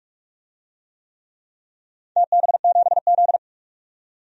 Morse code, heard as a keyed tone, about 46 wpm, 700 Hz. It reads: TB7Z
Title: DXCC-100 Call Signs - 4 Characters - 45wpm